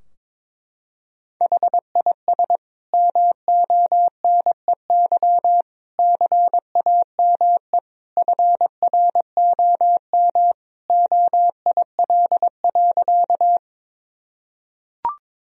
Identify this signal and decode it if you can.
Morse code, heard as a keyed tone, about 22 wpm, 700 Hz. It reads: HIS MONEY CAME FROM OIL.
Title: Sentences from Top 1000 Words - Review Part 4 - 22wpm